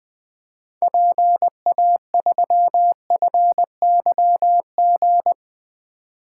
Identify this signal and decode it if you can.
Morse code, heard as a keyed tone, about 20 wpm, 700 Hz. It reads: PA3FYG
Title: 2023 SOTA Activator Call Signs - 20wpm